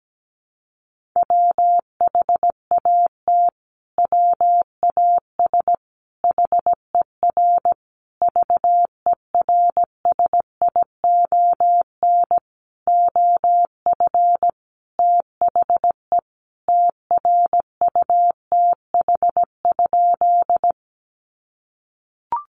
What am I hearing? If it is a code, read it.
Morse code, heard as a keyed tone, about 17 wpm, 700 Hz. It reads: WHAT WAS HER VERSION OF THE TRUTH?
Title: Sentences from Top 1400 Words 17wpm